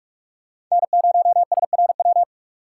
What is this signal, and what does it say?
Morse code, heard as a keyed tone, about 46 wpm, 700 Hz. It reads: N0SRW
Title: USA Call Signs - 5 Characters - 2x Character Spacing - 45wpm